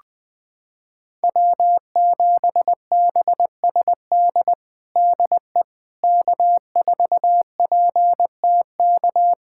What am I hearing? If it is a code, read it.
Morse code, heard as a keyed tone, about 19 wpm, 700 Hz. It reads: W7BSD DE K4PTK
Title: USA Callsign DE Callsign - 5 Characters - 20wpm